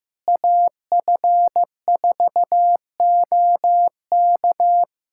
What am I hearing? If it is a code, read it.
Morse code, heard as a keyed tone, about 15 wpm, 700 Hz. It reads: AF4OK
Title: USA Callsign DE Callsign - 5 Characters - 15wpm